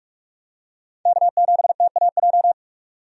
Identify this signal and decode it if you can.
Morse code, heard as a keyed tone, about 42 wpm, 700 Hz. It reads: K7TAJ
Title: USA Call Signs - 5 Characters - 45wpm